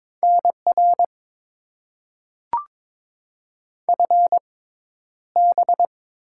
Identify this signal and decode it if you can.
Morse code, heard as a keyed tone, about 22 wpm, spaced as for 11 wpm, 700 Hz. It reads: NR FB
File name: Top 90 QSO Elements - 6x Character Spacing - 22wpm